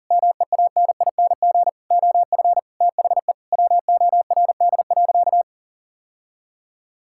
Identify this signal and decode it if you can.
Morse code, heard as a keyed tone, about 40 wpm, 700 Hz. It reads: MEANING OF THE WORD.
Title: Sentences from Top 1000 Words - Review Part 6 - 40wpm